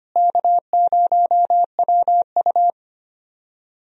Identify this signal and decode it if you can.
Morse code, heard as a keyed tone, about 25 wpm, 700 Hz. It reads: K0WU
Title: USA Call Signs - 4 Characters - Repeated 2x - 25wpm